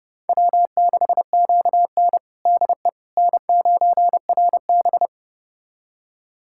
Morse code, heard as a keyed tone, about 32 wpm, 700 Hz. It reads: W6QN DE N9RB
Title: USA Callsign DE Callsign - 4 Characters - 30wpm